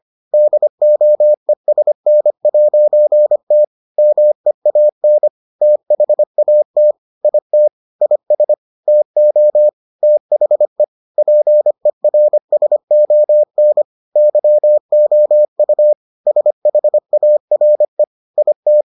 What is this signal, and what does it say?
Morse code, heard as a keyed tone, about 25 wpm, 600 Hz. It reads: DOESN'T MEAN THAT IT IS TO THE PERSON YOU SHARE IT